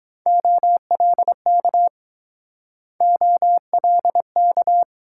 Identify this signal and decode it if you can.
Morse code, heard as a keyed tone, about 22 wpm, 700 Hz. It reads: OLK OLK